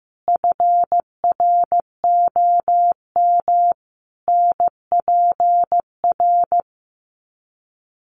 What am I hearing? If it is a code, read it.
Morse code, heard as a keyed tone, about 15 wpm, 700 Hz. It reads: FROM NPR
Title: Sets of 5 Words 15wpm